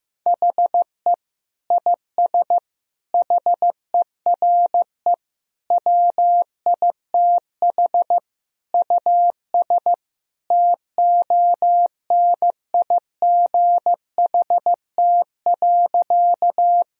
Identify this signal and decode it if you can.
Morse code, heard as a keyed tone, about 15 wpm, 700 Hz. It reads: HE IS HERE WITH US TONIGHT.
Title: Sentences from Top 1000 Words - Review Part 3 - 15wpm